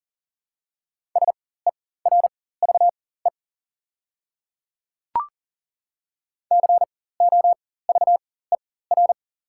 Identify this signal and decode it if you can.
Morse code, heard as a keyed tone, about 40 wpm, spaced as for 19 wpm, 690 Hz. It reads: SERVE COVER
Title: 4x Character Spacing - Top 500 Words - 40wpm